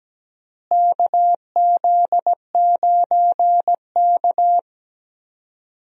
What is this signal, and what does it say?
Morse code, heard as a keyed tone, about 17 wpm, 700 Hz. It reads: KZ9K
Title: USA Call Signs - 4 Characters - Repeated 3x - 17wpm